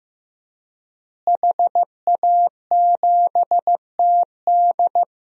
Morse code, heard as a keyed tone, about 15 wpm, 700 Hz. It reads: HA7TD
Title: DXCC-100 Call Signs - 5 Characters - Repeated 3x - 15wpm